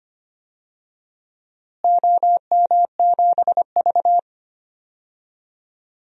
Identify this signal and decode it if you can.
Morse code, heard as a keyed tone, about 25 wpm, 700 Hz. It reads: OM7V
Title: DXCC-100 Call Signs - 4 Characters - Repeated 2x - 25wpm